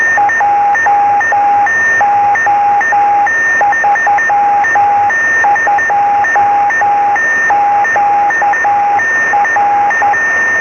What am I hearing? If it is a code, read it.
Morse code, heard as a keyed tone, about 10 wpm, 820 Hz, with noise. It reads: JO32QR
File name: sm7lcb / db0ot-b 23cm